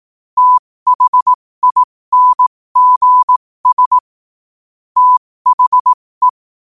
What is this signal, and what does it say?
Morse code, heard as a keyed tone, about 18 wpm, spaced as for 15 wpm, 1000 Hz. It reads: THINGS THE
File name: qotdfiles / 15wpm